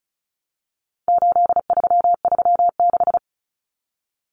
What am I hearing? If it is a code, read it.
Morse code, heard as a keyed tone, about 34 wpm, 700 Hz. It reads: 8336
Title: Numbers - 4-Digit 35wpm